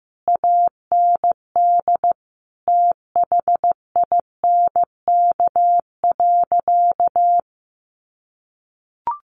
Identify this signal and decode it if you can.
Morse code, heard as a keyed tone, about 15 wpm, 700 Hz. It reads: AND THINK.
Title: Sentences from Top 500 Words 15wpm